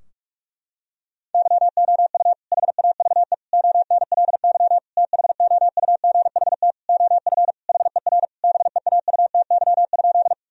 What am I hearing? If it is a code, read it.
Morse code, heard as a keyed tone, about 46 wpm, 700 Hz. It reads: YOU HAVE ONLY THOUGHT OF HER BEAUTY?
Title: Sentences from Top 1000 Words - Review Part 2 - 45wpm